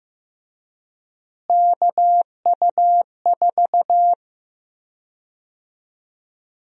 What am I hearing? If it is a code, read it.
Morse code, heard as a keyed tone, about 15 wpm, 700 Hz. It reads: KU4
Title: USA Call Sign Prefixes - Repeated 2x - 15wpm